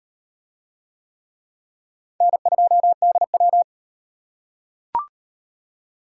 Morse code, heard as a keyed tone, about 36 wpm, 700 Hz. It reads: N2DW